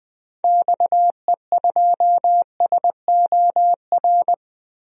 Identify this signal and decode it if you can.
Morse code, heard as a keyed tone, about 20 wpm, 700 Hz. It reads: XE2SOR
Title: DXCC-100 Call Signs - 6 Characters - Repeated 2x - 20wpm